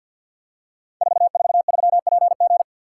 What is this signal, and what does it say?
Morse code, heard as a keyed tone, about 52 wpm, 700 Hz. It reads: 443PG
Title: Code Groups - 5 Letter:Numbers - 50wpm